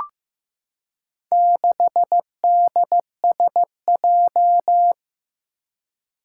Morse code, heard as a keyed tone, about 15 wpm, 700 Hz. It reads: 6DSJ
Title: Four Challenging Letter-Number Combinations 15wpm